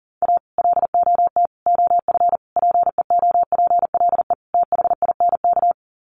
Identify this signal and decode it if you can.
Morse code, heard as a keyed tone, about 40 wpm, 700 Hz. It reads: A LOT OF PEOPLE THINK
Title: Sets of 5 Words Encore 40wpm